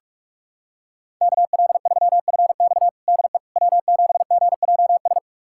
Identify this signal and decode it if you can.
Morse code, heard as a keyed tone, about 46 wpm, 700 Hz. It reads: KL3FX DE W7GJS
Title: USA Callsign DE Callsign - 5 Characters - 45wpm